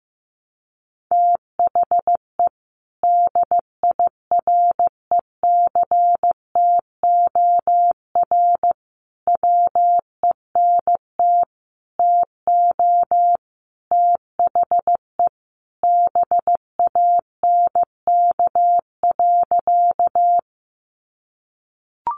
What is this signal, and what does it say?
Morse code, heard as a keyed tone, about 15 wpm, 700 Hz. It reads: THE DIRECTOR WENT TO THE BANK.
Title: Sentences from Top 1300 Words 15wpm